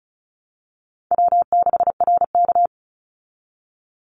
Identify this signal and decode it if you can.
Morse code, heard as a keyed tone, about 34 wpm, 700 Hz. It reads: W6RK
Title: Easy Call Signs Repeated 3x - 35wpm